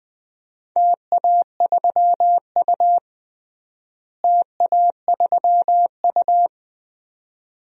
Morse code, heard as a keyed tone, about 20 wpm, 700 Hz. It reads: TA3U TA3U
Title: DXCC-100 Call Signs - 4 Characters - Repeated 2x - 20wpm